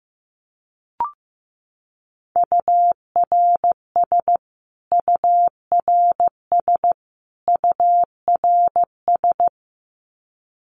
Morse code, heard as a keyed tone, about 15 wpm, 700 Hz. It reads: URS URS URS
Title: Abbreviations Repeated 3x - 15wpm